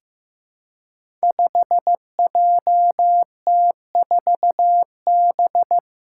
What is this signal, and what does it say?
Morse code, heard as a keyed tone, about 15 wpm, 700 Hz. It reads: 5JT4B
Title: Five Challenging Letter-Number Combinations 15wpm